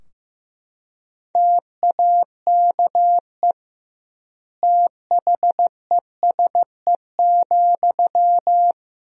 Morse code, heard as a keyed tone, about 16 wpm, 700 Hz. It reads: TAKE THESE,
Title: 2x Word-Spacing - Sentences from Top 300 Words - 15wpm